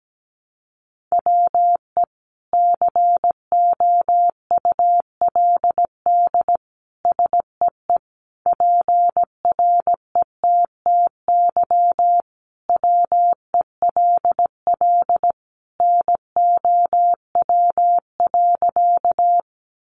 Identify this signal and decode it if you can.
Morse code, heard as a keyed tone, about 17 wpm, 700 Hz. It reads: WE COULD SEE PRETTY WELL NOW.
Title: Sentences from Top 1000 Words 17wpm